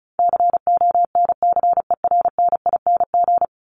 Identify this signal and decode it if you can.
Morse code, heard as a keyed tone, about 34 wpm, 700 Hz. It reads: CONCERNING
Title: Top 1751-2000 Words 35wpm